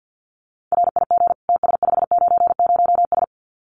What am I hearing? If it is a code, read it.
Morse code, heard as a keyed tone, about 50 wpm, 700 Hz. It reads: RIG TS590S
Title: Common QSO Phrases Encore 50wpm